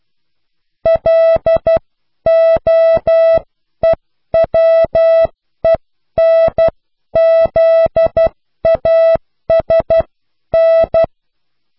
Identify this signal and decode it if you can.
Morse code, heard as a keyed tone, about 12 wpm, spaced as for 10 wpm, 660 Hz. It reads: LOEWENZASN
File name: cw / pseudo worden(10)